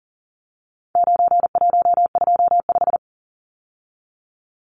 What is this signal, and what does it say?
Morse code, heard as a keyed tone, about 40 wpm, 700 Hz. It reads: 9125
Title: Numbers - 4-Digit 40wpm